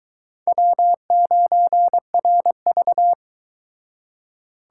Morse code, heard as a keyed tone, about 23 wpm, 700 Hz. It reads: W9RV